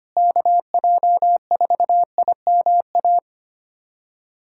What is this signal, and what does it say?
Morse code, heard as a keyed tone, about 25 wpm, 700 Hz. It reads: KJ4IMA
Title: USA Callsign DE Callsign - 6 Characters - 25wpm